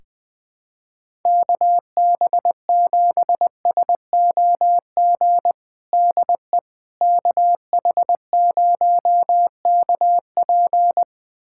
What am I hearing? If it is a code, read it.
Morse code, heard as a keyed tone, about 21 wpm, 700 Hz. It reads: KB7SOG DE KH0KP
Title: USA Callsign DE Callsign 20wpm